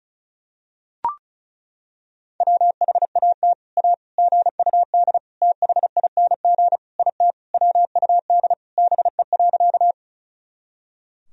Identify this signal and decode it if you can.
Morse code, heard as a keyed tone, about 34 wpm, 700 Hz. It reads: WHAT A GUD THING IT WUD BE.
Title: Sentences from Top 200 Words and Abbreviations 35wpm